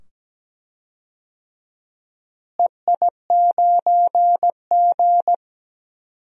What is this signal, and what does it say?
Morse code, heard as a keyed tone, about 17 wpm, 700 Hz. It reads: EI9G
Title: DXCC-100 Call Signs - 4 Characters - Repeated 2x - 17wpm